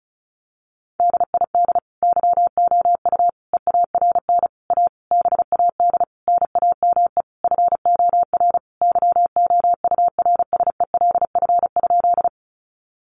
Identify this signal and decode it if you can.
Morse code, heard as a keyed tone, about 34 wpm, 700 Hz. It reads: DID YOU EARN A BAD NAME FOR YOURSELF?
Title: Sentences from Top 1500 Words 35wpm